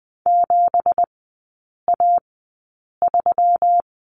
Morse code, heard as a keyed tone, about 20 wpm, spaced as for 8 wpm, 690 Hz. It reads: 7A3
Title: 172 Lesson 28e - Number 9 - Three Characters 20wpm